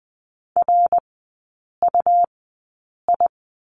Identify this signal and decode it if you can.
Morse code, heard as a keyed tone, about 20 wpm, spaced as for 8 wpm, 700 Hz. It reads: RUI
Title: 100 Lesson 17e - Letter u - Three Characters 20wpm